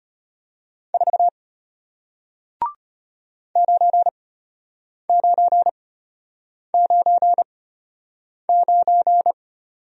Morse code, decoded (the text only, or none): 4 9999